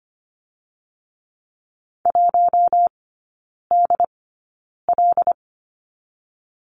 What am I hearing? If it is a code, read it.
Morse code, heard as a keyed tone, about 25 wpm, 700 Hz. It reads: 1 D L
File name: course / 25:10wpm / 100 Lesson 17e - Letter u - Three Characters 25wpm